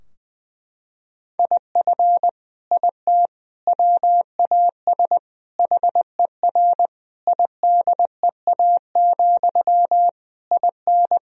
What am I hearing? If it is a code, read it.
Morse code, heard as a keyed tone, about 20 wpm, 700 Hz. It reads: IF IT WAS HER IDEA, IN
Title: Sentences from Top 1000 Words - Review Part 2 - 20wpm